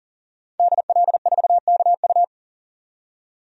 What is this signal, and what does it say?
Morse code, heard as a keyed tone, about 40 wpm, 700 Hz. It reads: DL4KU
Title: DXCC-100 Call Signs - 5 Characters - 40wpm